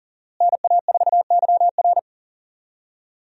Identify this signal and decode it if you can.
Morse code, heard as a keyed tone, about 40 wpm, 690 Hz. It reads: NA4YR
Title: USA Call Signs - 5 Characters - Repeated 3x - 40wpm